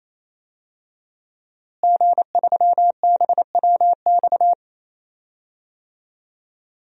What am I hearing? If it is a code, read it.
Morse code, heard as a keyed tone, about 28 wpm, 700 Hz. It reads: G3BWX
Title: DXCC-100 Call Signs - 5 Characters - Repeated 3x - 28wpm